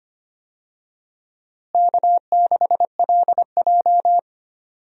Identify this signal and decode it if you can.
Morse code, heard as a keyed tone, about 25 wpm, 700 Hz. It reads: K6LJ
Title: USA Call Signs - 4 Characters - Repeated 3x - 25wpm